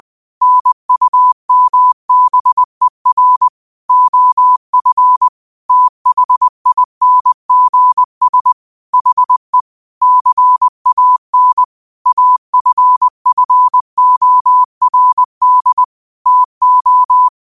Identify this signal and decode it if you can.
Morse code, heard as a keyed tone, about 20 wpm, 1000 Hz. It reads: NUMBER OF THINGS HE CAN AFFORD TO
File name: qotdfiles / 20wpm